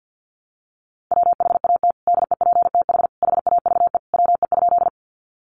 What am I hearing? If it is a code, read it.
Morse code, heard as a keyed tone, about 50 wpm, 700 Hz. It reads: WHAT DEPTH HAVE WE?
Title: Sentences from Top 1200 Words 50wpm